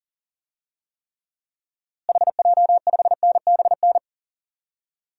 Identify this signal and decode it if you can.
Morse code, heard as a keyed tone, about 40 wpm, 700 Hz. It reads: HJ5NBN
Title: DXCC-100 Call Signs - 6 Characters - Repeated 2x - 40wpm